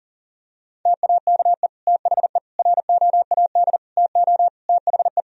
Morse code, heard as a keyed tone, about 37 wpm, 700 Hz. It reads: TAKE THE ROAD TO THE
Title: Sentences from Top 1000 Words - Review Part 2 - 40wpm